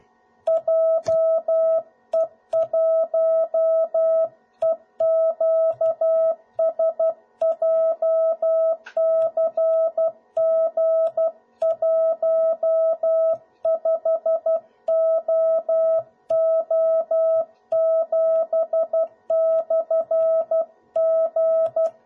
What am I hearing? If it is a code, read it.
Morse code, heard as a keyed tone, about 12 wpm, 650 Hz. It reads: JE1EQS JCG15OO7/G
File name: jh8cbh / 2021cw / cw01b